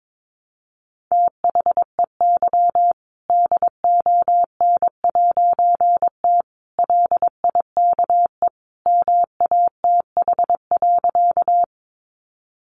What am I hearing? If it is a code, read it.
Morse code, heard as a keyed tone, about 22 wpm, 700 Hz. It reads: THEY DON'T LIKE MATH.
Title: Easy Phrases 22wpm